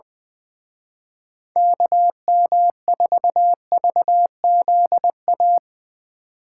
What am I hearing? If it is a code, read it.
Morse code, heard as a keyed tone, about 22 wpm, 700 Hz. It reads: KM4VZA
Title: Speed Racing - USA 2x3 Call Signs 20wpm